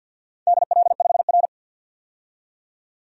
Morse code, heard as a keyed tone, about 51 wpm, 700 Hz. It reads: DL5R
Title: DXCC-100 Call Signs - 4 Characters - 50wpm